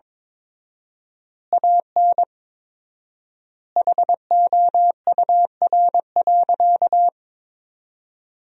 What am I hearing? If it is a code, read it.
Morse code, heard as a keyed tone, about 21 wpm, 690 Hz. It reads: AN HOUR.
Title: 4x Word-Spacing - Sentences from Top 400 Words - 22wpm